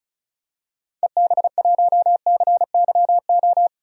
Morse code, heard as a keyed tone, about 36 wpm, 700 Hz. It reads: EB1CYO
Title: DXCC-100 Call Signs - 6 Characters - 35wpm